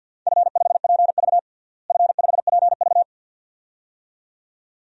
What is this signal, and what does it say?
Morse code, heard as a keyed tone, about 50 wpm, 700 Hz. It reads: F5PV F5PV
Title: DXCC-100 Call Signs - 4 Characters - Repeated 2x - 50wpm